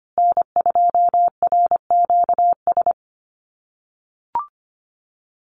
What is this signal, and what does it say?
Morse code, heard as a keyed tone, about 25 wpm, 700 Hz. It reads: N2RQS
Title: USA Call Signs Repeated 2x 25wpm